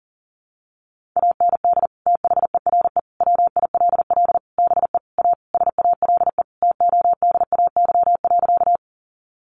Morse code, heard as a keyed tone, about 37 wpm, 700 Hz. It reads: AND THERE WILL BE A SALE TODAY.
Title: Sentences from Top 1400 Words 40wpm